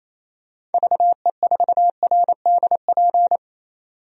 Morse code, heard as a keyed tone, about 28 wpm, 700 Hz. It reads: VE4RDP
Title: DXCC-100 Call Signs - 6 Characters - 28wpm